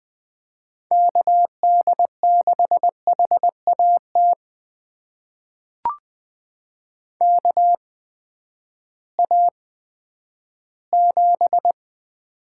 Morse code, heard as a keyed tone, about 20 wpm, spaced as for 11 wpm, 700 Hz. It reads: KD6HAT KA7
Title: USA Call Signs - 6 Characters - 8x Character Spacing - 20wpm